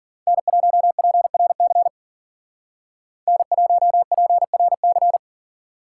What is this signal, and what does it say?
Morse code, heard as a keyed tone, about 48 wpm, 700 Hz. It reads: N1PRC N1PRC
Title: Speed Racing - USA Call Signs 40wpm